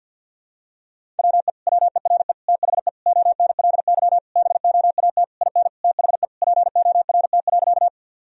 Morse code, heard as a keyed tone, about 50 wpm, 700 Hz. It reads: WE WERE THE ONLY BOAT IN THE PORT.
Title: Sentences from Top 1000 Words - Review Part 1 - 50wpm